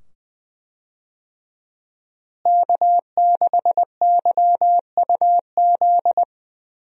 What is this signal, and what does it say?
Morse code, heard as a keyed tone, about 20 wpm, 700 Hz. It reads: K6YUZ
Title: USA Call Signs - 5 Characters - Repeated 2x - 20wpm